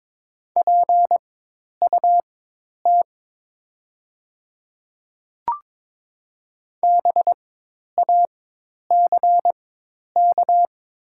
Morse code, heard as a keyed tone, about 22 wpm, spaced as for 10 wpm, 700 Hz. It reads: PUT BACK
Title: 4x Character Spacing - Top 500 Words - 22wpm